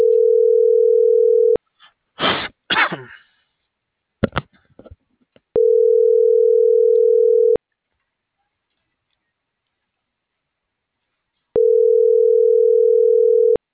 Ringback tone (North America), 440 + 480 Hz.